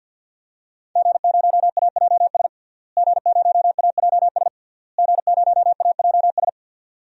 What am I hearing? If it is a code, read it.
Morse code, heard as a keyed tone, about 46 wpm, 700 Hz. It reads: G0AJS G0AJS G0AJS
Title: DXCC-100 Call Signs - 5 Characters - Repeated 3x - 50wpm